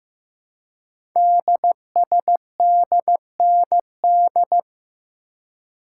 Morse code, heard as a keyed tone, about 15 wpm, 700 Hz. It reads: DSDND
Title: Code Groups - 5 Letters - 15wpm